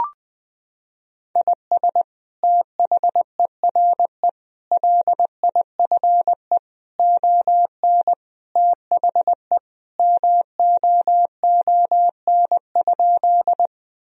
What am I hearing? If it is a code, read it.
Morse code, heard as a keyed tone, about 20 wpm, 700 Hz. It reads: IS THERE LIFE ON THE MOON?
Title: Sentences from Top 1000 Words - Review Part 3 - 20wpm